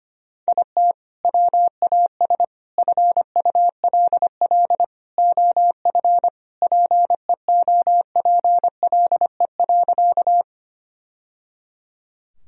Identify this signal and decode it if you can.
Morse code, heard as a keyed tone, about 25 wpm, 700 Hz. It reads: IT WAS FULL OF PEOPLE.
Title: Sentences from Top 1000 Words - Review Part 3 - 25wpm